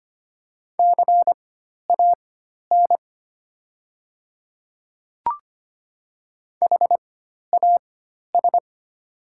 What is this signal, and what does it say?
Morse code, heard as a keyed tone, about 25 wpm, spaced as for 12 wpm, 700 Hz. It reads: CAN HAS
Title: 4x Character Spacing - Top 100 Words - 25wpm